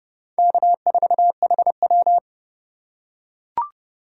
Morse code, heard as a keyed tone, about 26 wpm, 700 Hz. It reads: K4HW